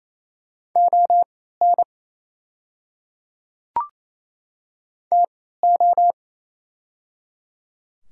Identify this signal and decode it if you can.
Morse code, heard as a keyed tone, about 28 wpm, spaced as for 16 wpm, 690 Hz. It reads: ON TO
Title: 3x Character Spacing - Top 100 Words - 28wpm